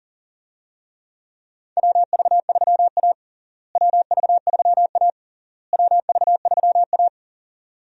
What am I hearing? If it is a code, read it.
Morse code, heard as a keyed tone, about 40 wpm, 700 Hz. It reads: WV3A WV3A WV3A
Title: USA Call Signs - 4 Characters - Repeated 3x - 40wpm